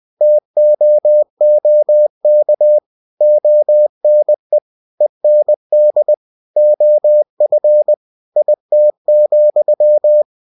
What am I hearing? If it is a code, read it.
Morse code, heard as a keyed tone, about 20 wpm, 600 Hz. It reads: TOOK ONE END OF IT,
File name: winnie-20wpm